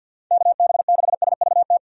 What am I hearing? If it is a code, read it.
Morse code, heard as a keyed tone, about 46 wpm, 700 Hz. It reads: KB6SVT
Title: USA Call Signs - 6 Characters - 50wpm